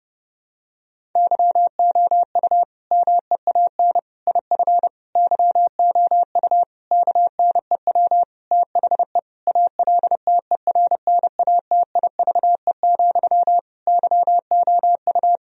Morse code, heard as a keyed tone, about 30 wpm, 700 Hz. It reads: YOU MEAN IF YOU KNEW THE ALTERNATIVE, YOU
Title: Sentences from Top 1000 Words - Review Part 4 - 30wpm